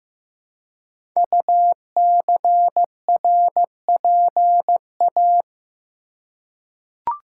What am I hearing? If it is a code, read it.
Morse code, heard as a keyed tone, about 15 wpm, 700 Hz. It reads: UCRPA
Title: Code Groups - 5 Letter:Numbers - 15wpm